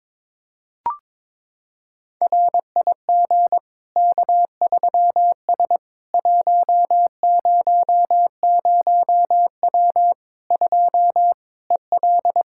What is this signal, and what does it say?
Morse code, heard as a keyed tone, about 21 wpm, 700 Hz. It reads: RIG K3S 100W 2 EL
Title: Practice QSO - Exchange 2 - Info - 22wpm